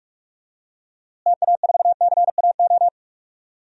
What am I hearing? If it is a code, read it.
Morse code, heard as a keyed tone, about 46 wpm, 700 Hz. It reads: TA4CAO
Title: DXCC-100 Call Signs - 6 Characters - Repeated 2x - 45wpm